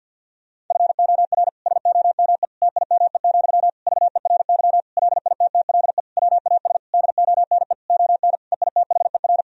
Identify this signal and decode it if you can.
Morse code, heard as a keyed tone, about 50 wpm, 700 Hz. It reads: FOR SOME TIME, VERY LITTLE WAS DONE ON EITHER
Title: Sentences from Top 1000 Words - Review Part 4 - 50wpm